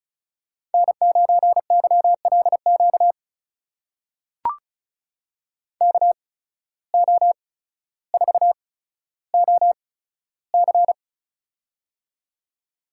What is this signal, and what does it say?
Morse code, heard as a keyed tone, about 36 wpm, spaced as for 15 wpm, 700 Hz. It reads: N9YLQ KO4OC
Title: USA Call Signs - 5 Characters - 8x Character Spacing - 35wpm